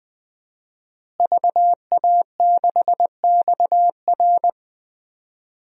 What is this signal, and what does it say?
Morse code, heard as a keyed tone, about 20 wpm, 700 Hz. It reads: VA6XR
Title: DXCC-100 Call Signs - 5 Characters - Repeated 3x - 20wpm